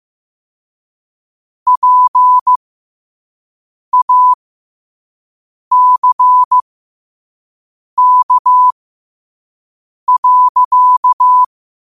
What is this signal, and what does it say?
Morse code, heard as a keyed tone, about 15 wpm, spaced as for 5 wpm, 1000 Hz. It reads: PACK.
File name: qotdfiles / 5wpm